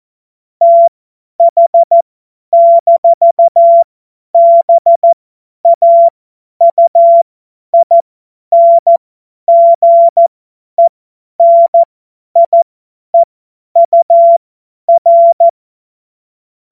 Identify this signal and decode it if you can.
Morse code, heard as a keyed tone, about 14 wpm, spaced as for 10 wpm, 680 Hz. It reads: TH-BAUINGENIEUR
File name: Morsekurs / Witze / Witz 31. Mai 2022 50 F70